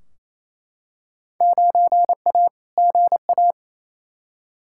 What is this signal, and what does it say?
Morse code, heard as a keyed tone, about 28 wpm, 700 Hz. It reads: 9A GA
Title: Field Day Exchanges Encore - 28wpm